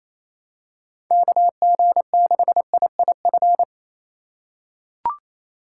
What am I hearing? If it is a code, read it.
Morse code, heard as a keyed tone, about 28 wpm, 700 Hz. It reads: KG6IIF
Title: USA Call Signs - 6 Characters - 2x Character Spacing - 28wpm